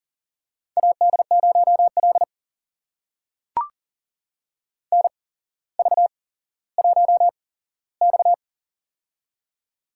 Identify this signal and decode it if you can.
Morse code, heard as a keyed tone, about 40 wpm, spaced as for 17 wpm, 700 Hz. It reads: AD0L NV1X
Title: USA Call Signs - 4 Characters - 8x Character Spacing - 40wpm